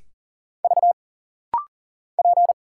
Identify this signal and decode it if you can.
Morse code, heard as a keyed tone, about 40 wpm, 700 Hz. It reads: V P
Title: Single Letters - Rapid-Fire 40wpm